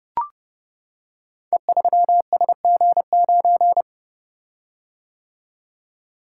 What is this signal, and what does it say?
Morse code, heard as a keyed tone, about 30 wpm, 700 Hz. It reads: E3SG9
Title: Code Groups - 5 Letter:Numbers - 30wpm